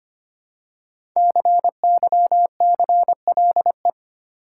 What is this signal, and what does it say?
Morse code, heard as a keyed tone, about 25 wpm, 700 Hz. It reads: CYCLE
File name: 5 Letter Words - First 3 Letters Given - 25wpm